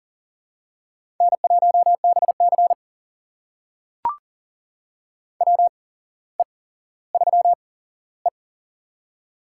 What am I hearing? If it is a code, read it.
Morse code, heard as a keyed tone, about 40 wpm, spaced as for 17 wpm, 700 Hz. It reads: N1BC WE3E
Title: USA Call Signs - 4 Characters - 8x Character Spacing - 40wpm